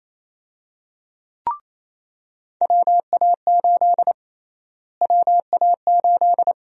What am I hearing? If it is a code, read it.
Morse code, heard as a keyed tone, about 28 wpm, 700 Hz. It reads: WA8 WA8
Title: USA Call Sign Prefixes - Repeated 2x - 28wpm